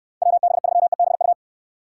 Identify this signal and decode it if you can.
Morse code, heard as a keyed tone, about 69 wpm, 700 Hz. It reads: WB3EBU